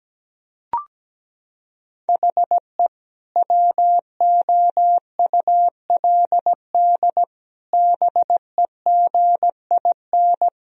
Morse code, heard as a keyed tone, about 17 wpm, 700 Hz. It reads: HE WOULD BEGIN
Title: Sentences from Top 1000 Words - Review Part 1 - 17wpm